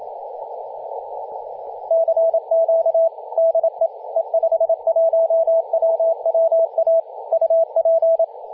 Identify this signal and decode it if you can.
Morse code, heard as a keyed tone, about 27 wpm, 660 Hz, with radio noise. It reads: CQ DE E51WWA UP
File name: e51wwa-10-104mhz-cw